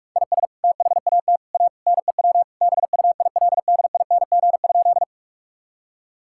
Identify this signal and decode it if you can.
Morse code, heard as a keyed tone, about 46 wpm, 700 Hz. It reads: IS THAT A NEW BUILDING?
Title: Sentences from Top 1000 Words - Review Part 5 - 45wpm